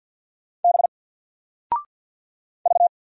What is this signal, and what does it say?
Morse code, heard as a keyed tone, about 50 wpm, 700 Hz. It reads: B V
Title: Single Letter-Number 50wpm